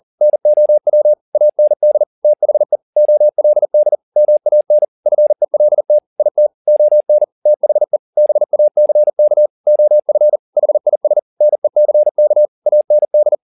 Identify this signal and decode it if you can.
Morse code, heard as a keyed tone, about 40 wpm, 600 Hz. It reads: NOW AND THE OLD MAN FELT IT ON THE BACK OF HIS NECK AND